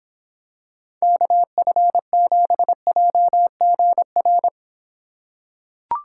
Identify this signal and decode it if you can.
Morse code, heard as a keyed tone, about 26 wpm, 700 Hz. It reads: KF7JGR